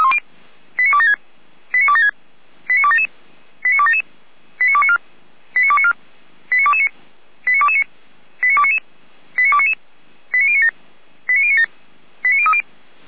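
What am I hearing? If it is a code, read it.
Five-tone selective call (ZVEI), heard as quick runs of five tones, tones of about 70 ms. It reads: 89276 89276 89277 89277 89294 89294 89229 89229 89299 89299 89097 89097 89030
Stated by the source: SDR-received ZVEI five-tone selective call, address 89265.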